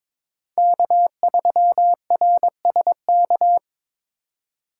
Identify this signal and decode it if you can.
Morse code, heard as a keyed tone, about 21 wpm, 700 Hz. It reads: K3RSK
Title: USA Call Signs - 5 Characters - 4x Character Spacing - 22wpm